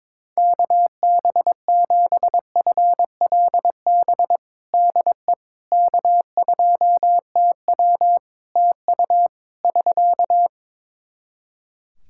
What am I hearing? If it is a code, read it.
Morse code, heard as a keyed tone, about 22 wpm, 700 Hz. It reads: KB7FLB DE K2TW TU <SK>
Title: Practice QSO - Exchange 4 - Ending - 22wpm